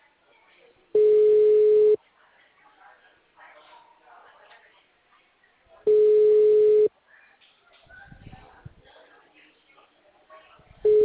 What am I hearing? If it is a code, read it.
Ringback tone (Europe, 425 Hz), with speech.